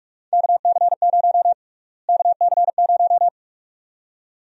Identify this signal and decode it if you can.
Morse code, heard as a keyed tone, about 47 wpm, 700 Hz. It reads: KC0 KC0
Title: USA Call Sign Prefixes - Repeated 2x - 45wpm